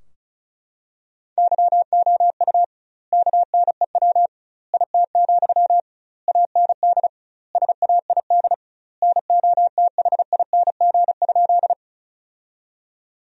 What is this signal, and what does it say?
Morse code, heard as a keyed tone, about 34 wpm, 700 Hz. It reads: YOU KNEW IT, AND SAID NOTHING?
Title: 2x Word-Spacing - Sentences from Top 500 Words - 35wpm